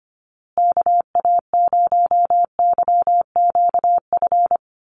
Morse code, heard as a keyed tone, about 25 wpm, 700 Hz. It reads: KA0YQF
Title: USA Call Signs Repeated 2x 25wpm